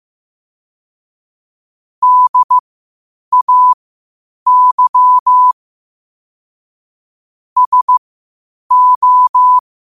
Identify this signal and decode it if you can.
Morse code, heard as a keyed tone, about 15 wpm, spaced as for 8 wpm, 1000 Hz. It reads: DAY SO
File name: qotdfiles / 8wpm